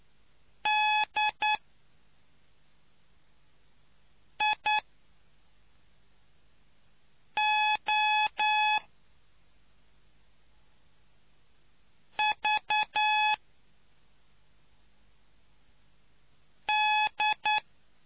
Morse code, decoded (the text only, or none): DIOVD